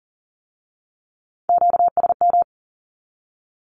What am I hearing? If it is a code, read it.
Morse code, heard as a keyed tone, about 41 wpm, 700 Hz. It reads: QSM
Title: Q-Codes with Spoken Meaning - 40wpm